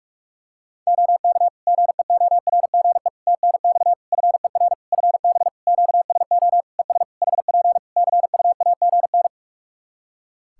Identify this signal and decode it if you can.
Morse code, heard as a keyed tone, about 46 wpm, 700 Hz. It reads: OK GEORGE TNX FER FB QSO ES HP CUAGN
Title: K1USN SST QSO Practice 45wpm